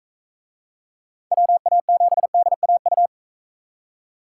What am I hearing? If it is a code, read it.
Morse code, heard as a keyed tone, about 41 wpm, 700 Hz. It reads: WA7DAU